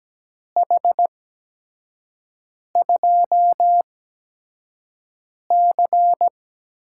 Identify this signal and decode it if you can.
Morse code, heard as a keyed tone, about 17 wpm, spaced as for 5 wpm, 700 Hz. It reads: H2C
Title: USA Call Signs - 5 Characters - 8x Character Spacing - 17wpm